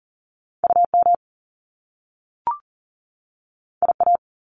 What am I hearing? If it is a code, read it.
Morse code, heard as a keyed tone, about 38 wpm, 700 Hz. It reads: UM IA
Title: Top Two Letter Combinations in Words 40wpm